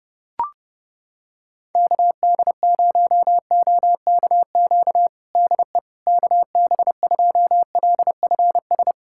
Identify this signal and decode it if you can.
Morse code, heard as a keyed tone, about 30 wpm, 700 Hz. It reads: KD0OKQ DE KB2LFS
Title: USA Callsign DE Callsign - 6 Characters - 30wpm